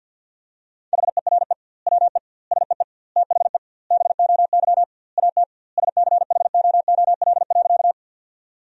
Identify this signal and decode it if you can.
Morse code, heard as a keyed tone, about 50 wpm, 700 Hz. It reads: HERE WE SEE THE BOY AT SCHOOL.
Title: 2x Word-Spacing - Sentences from Top 300 Words - 50wpm